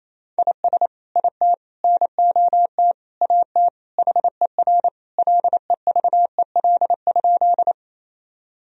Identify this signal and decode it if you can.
Morse code, heard as a keyed tone, about 28 wpm, 700 Hz. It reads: IS IT NOT AT HER LEVEL?
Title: Sentences from Top 1000 Words - Review Part 4 - 28wpm